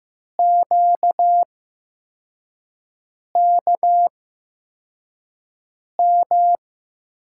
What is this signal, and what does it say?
Morse code, heard as a keyed tone, about 15 wpm, spaced as for 4 wpm, 700 Hz. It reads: QKM
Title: USA Call Signs - 6 Characters - 8x Character Spacing - 15wpm